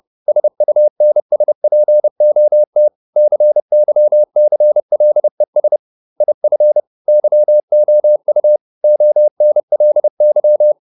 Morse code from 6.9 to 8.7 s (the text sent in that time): YOU